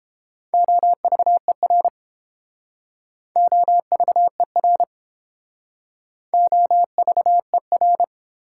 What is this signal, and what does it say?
Morse code, heard as a keyed tone, about 29 wpm, 700 Hz. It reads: OVER OVER OVER